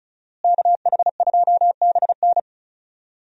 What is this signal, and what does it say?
Morse code, heard as a keyed tone, about 34 wpm, 700 Hz. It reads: KH2BN
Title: USA Call Signs - 5 Characters - Repeated 2x - 35wpm